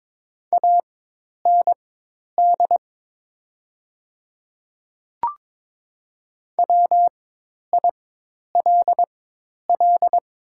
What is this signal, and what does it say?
Morse code, heard as a keyed tone, about 22 wpm, spaced as for 10 wpm, 700 Hz. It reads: AND WILL
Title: 4x Character Spacing - Top 100 Words - 22wpm